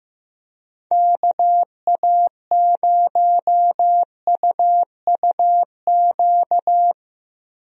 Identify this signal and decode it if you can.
Morse code, heard as a keyed tone, about 15 wpm, 700 Hz. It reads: KA0UUQ
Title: USA Call Signs - 6 Characters - Repeated 3x - 15wpm